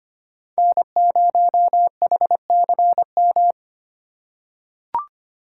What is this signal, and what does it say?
Morse code, heard as a keyed tone, about 25 wpm, 700 Hz. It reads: N0HCM
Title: USA Call Signs - 5 Characters - 3x Character Spacing - 25wpm